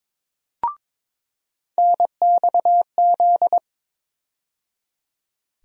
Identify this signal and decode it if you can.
Morse code, heard as a keyed tone, about 22 wpm, 700 Hz. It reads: NXZ
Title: USA Call Sign Postfixes - 3 Letters 22wpm